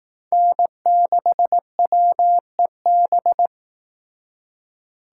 Morse code, heard as a keyed tone, about 18 wpm, 700 Hz. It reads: N6WEB
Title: Speed Racing - USA Call Signs 15wpm